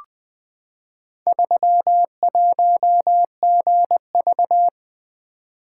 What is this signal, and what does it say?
Morse code, heard as a keyed tone, about 20 wpm, 700 Hz. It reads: 31GV
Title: Four Letter-Number Combinations - 20wpm